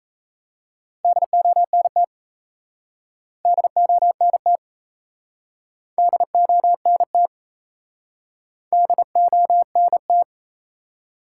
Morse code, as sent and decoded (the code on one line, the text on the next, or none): -.. --- -. - / -.. --- -. - / -.. --- -. - / -.. --- -. -
DONT DONT DONT DONT